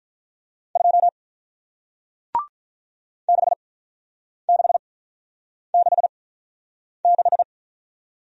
Morse code, heard as a keyed tone, about 48 wpm, spaced as for 9 wpm, 700 Hz. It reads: 2 6666